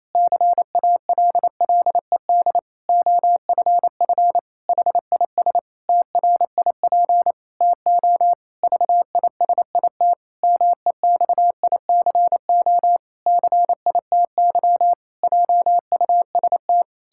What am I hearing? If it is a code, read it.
Morse code, heard as a keyed tone, about 28 wpm, 700 Hz. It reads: CALLED OFF HIS TRIP TO VISIT MEXICO CITY JUST
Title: Pangram Challenge v2 - 28wpm